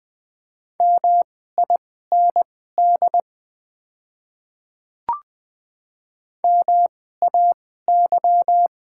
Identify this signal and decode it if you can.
Morse code, heard as a keyed tone, about 20 wpm, spaced as for 14 wpm, 700 Hz. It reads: MIND MAY
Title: 2x Character Spacing - Top 500 Words - 20wpm